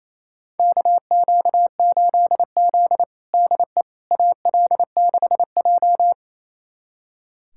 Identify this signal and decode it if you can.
Morse code, heard as a keyed tone, about 28 wpm, 700 Hz. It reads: KQ8Z DE AL6J
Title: USA Callsign DE Callsign - 4 Characters - 28wpm